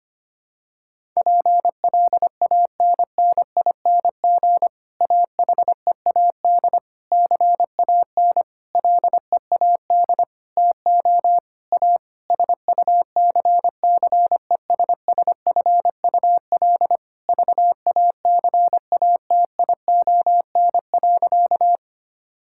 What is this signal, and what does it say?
Morse code, heard as a keyed tone, about 25 wpm, 700 Hz. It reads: PLANNING AHEAD CAN LEAD TO A SUCCESSFUL VACATION.
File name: practice / 25wpm / Difficult Sentences with Hints - 25wpm